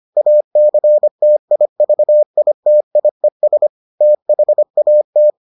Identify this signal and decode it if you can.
Morse code, heard as a keyed tone, about 25 wpm, 600 Hz. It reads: ACTIVITIES THAT